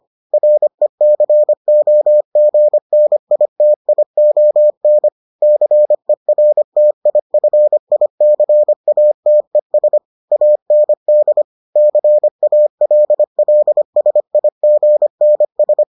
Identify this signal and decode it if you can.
Morse code, heard as a keyed tone, about 25 wpm, 600 Hz. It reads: RECOGNITION CERTIFICATES AND CALLSIGNS